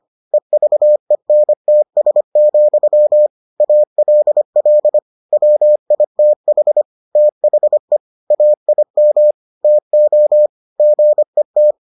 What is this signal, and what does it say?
Morse code, heard as a keyed tone, about 25 wpm, 600 Hz. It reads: EVENTS, ALL WITH THE AIM TO GET